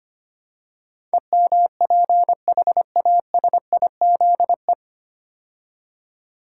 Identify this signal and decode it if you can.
Morse code, heard as a keyed tone, about 25 wpm, 690 Hz. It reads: EMPHASIZE
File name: practice / 25wpm / 9 Letter Words - First 3 Letters Given - 25wpm